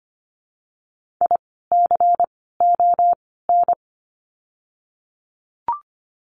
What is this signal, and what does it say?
Morse code, heard as a keyed tone, about 25 wpm, 700 Hz. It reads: I C O N
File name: course / 25:10wpm / 094 Lesson 16f - Letter c - Words 25wpm